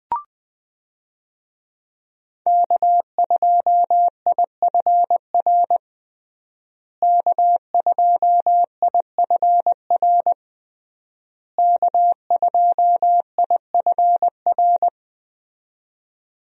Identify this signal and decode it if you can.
Morse code, heard as a keyed tone, about 21 wpm, 700 Hz. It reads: K2IFR K2IFR K2IFR
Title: USA Call Signs - 5 Characters - Repeated 3x - 20wpm